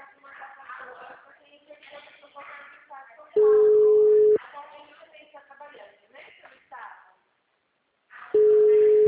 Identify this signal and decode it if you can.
Ringback tone (Europe, 425 Hz), with speech.
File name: out- -2029- - - .58158